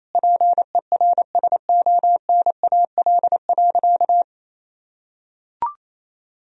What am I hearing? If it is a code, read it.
Morse code, heard as a keyed tone, about 28 wpm, 700 Hz. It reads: PERSONAL.
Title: Sentences from Top 1000 Words - Review Part 4 - 28wpm